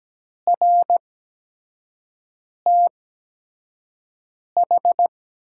Morse code, heard as a keyed tone, about 17 wpm, spaced as for 5 wpm, 700 Hz. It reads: RTH
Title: 8x Character Spacing - Top 500 Words - 17wpm